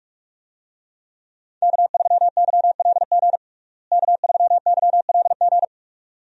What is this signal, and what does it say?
Morse code, heard as a keyed tone, about 46 wpm, 700 Hz. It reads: K3YLG K3YLG
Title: USA Call Signs - 5 Characters - Repeated 2x - 45wpm